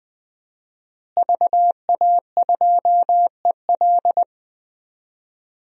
Morse code, heard as a keyed tone, about 20 wpm, 700 Hz. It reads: VA2EL
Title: DXCC-100 Call Signs - 5 Characters - Repeated 2x - 20wpm